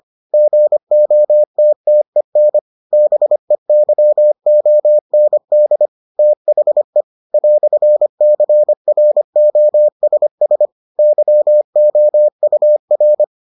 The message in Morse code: GOTTEN BEYOND THE "CROSS YOUR

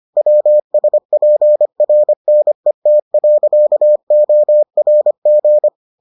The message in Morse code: WSPRNET.ORG

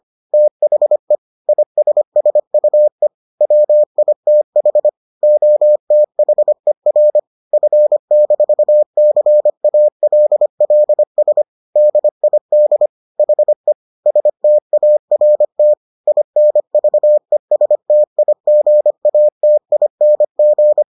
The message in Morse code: THE ISSUE WITH OTHER F-CALLS DID HE START INVESTIGATING

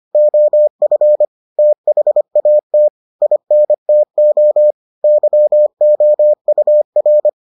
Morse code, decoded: OF THAT INTO YOUR

